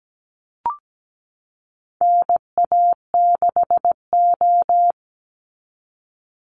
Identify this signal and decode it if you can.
Morse code, heard as a keyed tone, about 17 wpm, 700 Hz. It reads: NA6O
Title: NA SOTA Chaser Callsigns 17wpm